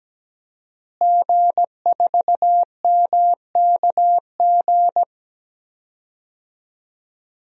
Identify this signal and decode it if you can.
Morse code, heard as a keyed tone, about 17 wpm, 700 Hz. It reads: G4MKG
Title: DXCC-100 Call Signs - 5 Characters - Repeated 3x - 17wpm